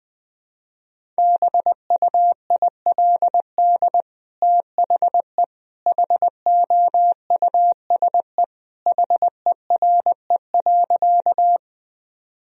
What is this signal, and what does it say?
Morse code, heard as a keyed tone, about 21 wpm, 700 Hz. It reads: BUILD THE HOUSE HERE.
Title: Sentences from Top 1000 Words - Review Part 1 - 20wpm